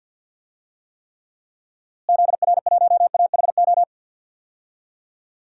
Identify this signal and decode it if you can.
Morse code, heard as a keyed tone, about 53 wpm, 700 Hz. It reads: ZR1AHO
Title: DXCC-100 Call Signs - 6 Characters - Repeated 2x - 50wpm